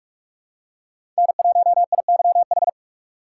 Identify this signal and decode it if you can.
Morse code, heard as a keyed tone, about 46 wpm, 700 Hz. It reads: N1IYH
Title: USA Call Signs - 5 Characters - 6x Character Spacing - 45wpm